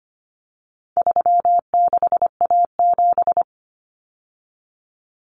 Morse code, heard as a keyed tone, about 25 wpm, 700 Hz. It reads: 36A7
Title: Common Cut Numbers - 4-Digit 25wpm